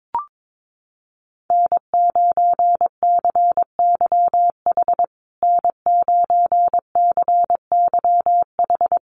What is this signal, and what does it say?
Morse code, heard as a keyed tone, about 22 wpm, 700 Hz. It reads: N9CYH N9CYH
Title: USA Call Signs Repeated 3x 22wpm